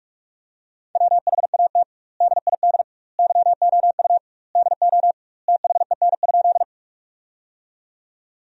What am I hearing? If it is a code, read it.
Morse code, heard as a keyed tone, about 41 wpm, 700 Hz. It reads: WHAT DID YOU DO THEN?
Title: 2x Word-Spacing - Sentences from Top 100 Words - 45wpm